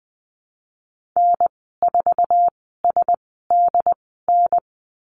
Morse code, heard as a keyed tone, about 20 wpm, spaced as for 15 wpm, 700 Hz. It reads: N4SDN
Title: 069 Lesson 12g - Letter d - Call Signs 20wpm